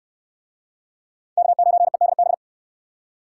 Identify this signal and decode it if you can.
Morse code, heard as a keyed tone, about 62 wpm, 700 Hz. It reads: K9EDB